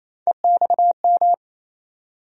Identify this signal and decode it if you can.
Morse code, heard as a keyed tone, about 28 wpm, 700 Hz. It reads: EXM
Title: USA Call Sign Postfixes - 3 Letters 28wpm